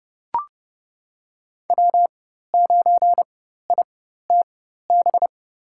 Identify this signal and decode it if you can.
Morse code, heard as a keyed tone, about 30 wpm, spaced as for 14 wpm, 700 Hz. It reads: W9ITB
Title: USA Call Signs - 5 Characters - 4x Character Spacing - 30wpm